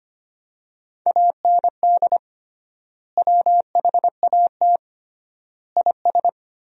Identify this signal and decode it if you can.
Morse code, heard as a keyed tone, about 25 wpm, 700 Hz. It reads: AND WHAT IS
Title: 3x Word-Spacing - Sentences from Top 100 Words - 25wpm